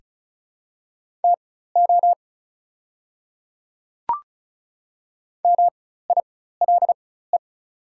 Morse code, heard as a keyed tone, about 36 wpm, spaced as for 16 wpm, 690 Hz. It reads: TO MILE
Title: 4x Character Spacing - Top 300 Words - 35wpm